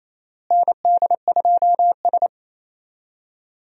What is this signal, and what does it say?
Morse code, heard as a keyed tone, about 27 wpm, 700 Hz. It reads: ND2S
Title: USA Call Signs - 4 Characters - Repeated 3x - 28wpm